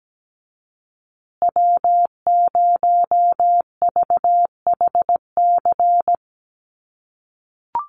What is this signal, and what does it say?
Morse code, heard as a keyed tone, about 17 wpm, 700 Hz. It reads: W0VHC
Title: USA Call Signs Repeated 3x 17wpm